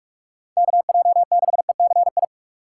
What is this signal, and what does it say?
Morse code, heard as a keyed tone, about 46 wpm, 700 Hz. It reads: KJ6ECI
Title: USA Call Signs - 6 Characters - 4x Character Spacing - 45wpm